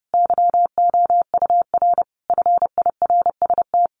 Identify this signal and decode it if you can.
Morse code, heard as a keyed tone, about 30 wpm, 700 Hz. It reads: YOUR FIRST
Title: Sentences from Top 1500 Words 30wpm